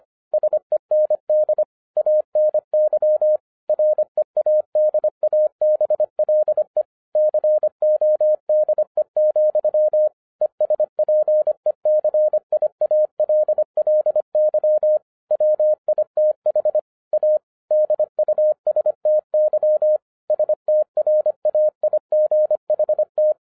Morse code, heard as a keyed tone, about 25 wpm, 610 Hz. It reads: SEND ANY READABLE CODE, ESPECIALLY WITH A DUSTY STRAIGHT
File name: FIRST FISTS CONTACT-25wpm